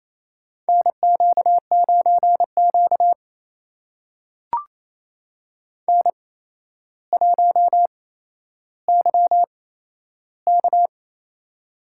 Morse code, heard as a keyed tone, about 28 wpm, spaced as for 12 wpm, 700 Hz. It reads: NQ9Q N1YK
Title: USA Call Signs - 4 Characters - 8x Character Spacing - 28wpm